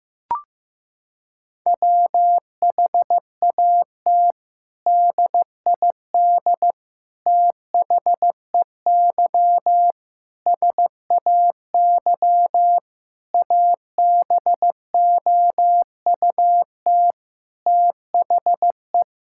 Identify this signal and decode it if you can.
Morse code, heard as a keyed tone, about 15 wpm, 700 Hz. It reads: WHAT DID THEY SAY ABOUT THE
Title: Sentences from Top 1000 Words - Review Part 5 - 15wpm